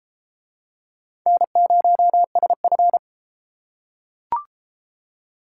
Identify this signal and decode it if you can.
Morse code, heard as a keyed tone, about 31 wpm, 700 Hz. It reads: N0SF